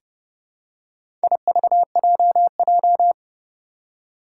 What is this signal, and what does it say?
Morse code, heard as a keyed tone, about 31 wpm, 700 Hz. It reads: IVJJ
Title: Four Challenging Letter-Number Combinations 30wpm